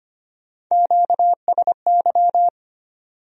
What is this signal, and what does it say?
Morse code, heard as a keyed tone, about 33 wpm, 700 Hz. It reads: QSY